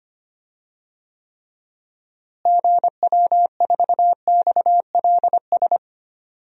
Morse code, heard as a keyed tone, about 25 wpm, 700 Hz. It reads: GW4XLS
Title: DXCC-100 Call Signs - 6 Characters - Repeated 3x - 25wpm